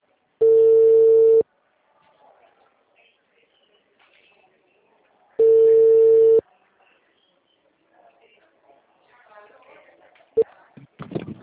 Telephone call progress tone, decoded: ringback tone

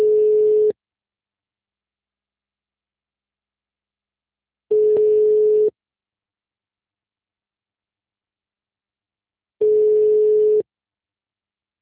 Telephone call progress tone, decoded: ringback tone

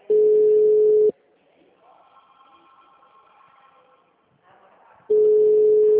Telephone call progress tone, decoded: ringback tone

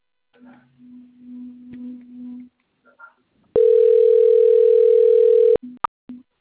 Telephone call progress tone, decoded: ringback tone

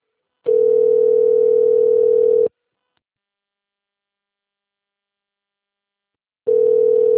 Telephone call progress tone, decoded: ringback tone

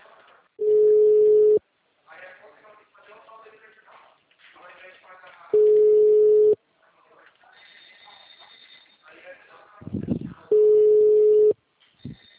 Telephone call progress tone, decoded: ringback tone

